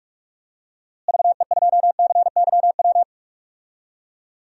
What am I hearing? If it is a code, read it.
Morse code, heard as a keyed tone, about 46 wpm, 700 Hz. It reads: VE2CYW
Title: DXCC-100 Call Signs - 6 Characters - Repeated 3x - 45wpm